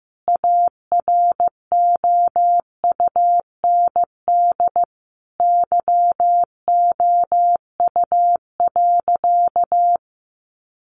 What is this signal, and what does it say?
Morse code, heard as a keyed tone, about 15 wpm, 700 Hz. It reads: AROUND YOU.
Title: Sentences from Top 1200 Words 15wpm